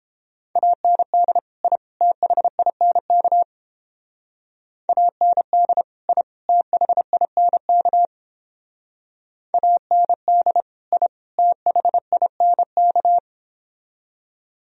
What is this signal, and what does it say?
Morse code, heard as a keyed tone, about 30 wpm, 700 Hz. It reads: AND I THINK AND I THINK AND I THINK